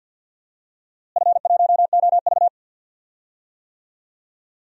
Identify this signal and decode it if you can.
Morse code, heard as a keyed tone, about 51 wpm, 700 Hz. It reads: F1OV
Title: DXCC-100 Call Signs - 4 Characters - Repeated 3x - 50wpm